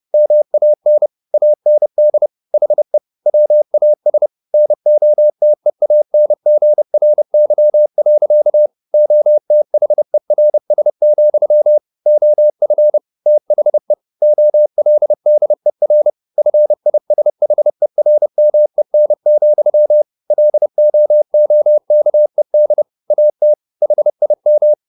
Morse code, heard as a keyed tone, about 30 wpm, 600 Hz. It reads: MAN AND HE WAS NOTEANGRY. OTHERS, OF THE OLDER FISHERMEN, LOOKED AT HIM
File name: the old man and the sea / the old man and the sea 1 30wpm